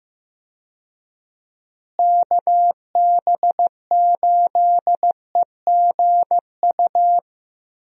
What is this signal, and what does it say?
Morse code, heard as a keyed tone, about 15 wpm, 700 Hz. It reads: KB8EGU
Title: USA Call Signs - 6 Characters - Repeated 3x - 15wpm